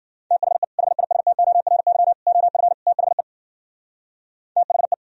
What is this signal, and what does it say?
Morse code, heard as a keyed tone, about 52 wpm, 700 Hz. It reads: THE HISTORY OF THE THE